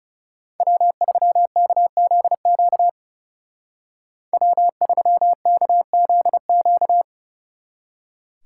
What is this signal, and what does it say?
Morse code, heard as a keyed tone, about 37 wpm, 700 Hz. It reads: W3KZQ W3KZQ
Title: Speed Racing - USA 1x3 Call Signs 30wpm